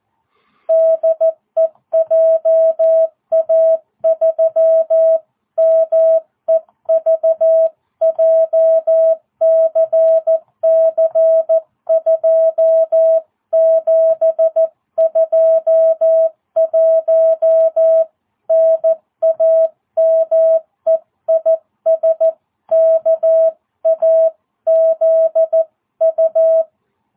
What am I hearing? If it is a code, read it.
Morse code, heard as a keyed tone, about 14 wpm, 660 Hz. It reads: DE JA3MEV JCC 2721 NAME IS KAZU